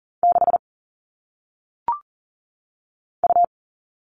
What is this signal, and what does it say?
Morse code, heard as a keyed tone, about 40 wpm, 700 Hz. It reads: 6 U
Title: All Cut Numbers - 1-Digit 40wpm